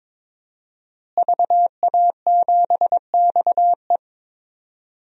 Morse code, heard as a keyed tone, about 22 wpm, 700 Hz. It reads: VA7XE
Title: DXCC-100 Call Signs - 5 Characters - Repeated 3x - 22wpm